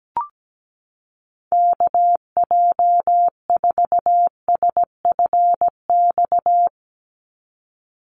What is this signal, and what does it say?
Morse code, heard as a keyed tone, about 17 wpm, 700 Hz. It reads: KJ4SFX
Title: USA Call Signs 17wpm